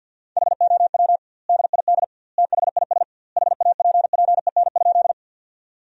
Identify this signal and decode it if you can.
Morse code, heard as a keyed tone, about 50 wpm, 700 Hz. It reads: HOW DID THIS HAPPEN?
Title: 2x Word-Spacing - Sentences from Top 400 Words - 50wpm